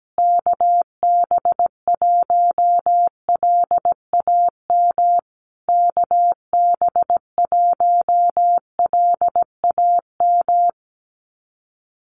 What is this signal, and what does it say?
Morse code, heard as a keyed tone, about 17 wpm, 700 Hz. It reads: KB1LAM KB1LAM
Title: USA Call Signs Repeated 3x 17wpm